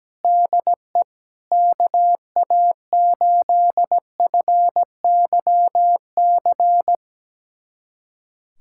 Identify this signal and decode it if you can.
Morse code, heard as a keyed tone, about 17 wpm, 700 Hz. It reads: DE KA8FYC
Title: USA Callsign DE Callsign - 6 Characters - 17wpm